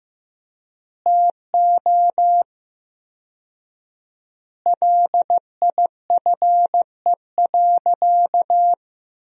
Morse code, heard as a keyed tone, about 15 wpm, 700 Hz. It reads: TO LIFE.
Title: 4x Word-Spacing - Sentences from Top 400 Words - 15wpm